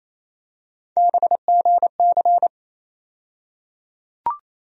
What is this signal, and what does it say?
Morse code, heard as a keyed tone, about 28 wpm, 700 Hz. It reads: BGC
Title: USA Call Sign Postfixes - 3 Letters 28wpm